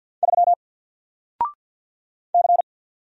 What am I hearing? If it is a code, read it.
Morse code, heard as a keyed tone, about 50 wpm, 700 Hz. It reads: 3 C
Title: Single Letter-Number 50wpm